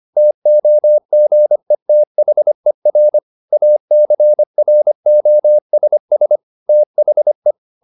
Morse code, heard as a keyed tone, about 25 wpm, 600 Hz. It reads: TOGETHER ACROSS THE